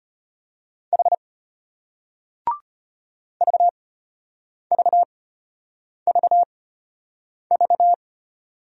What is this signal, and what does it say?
Morse code, heard as a keyed tone, about 33 wpm, spaced as for 8 wpm, 700 Hz. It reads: H VVVV